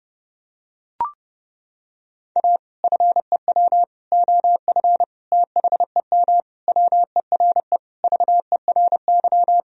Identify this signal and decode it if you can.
Morse code, heard as a keyed tone, about 32 wpm, 700 Hz. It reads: A FEW OF THEM WERE VERY
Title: Sentences from Top 1000 Words - Review Part 1 - 30wpm